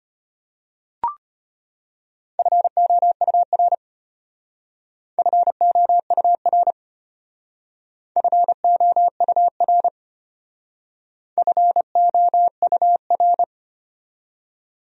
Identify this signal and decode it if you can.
Morse code, heard as a keyed tone, about 32 wpm, 700 Hz. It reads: FOUR FOUR FOUR FOUR